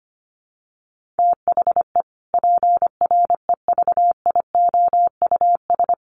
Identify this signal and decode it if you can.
Morse code, heard as a keyed tone, about 25 wpm, 700 Hz. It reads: THE PREVIOUS
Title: Sentences from Top 1500 Words 25wpm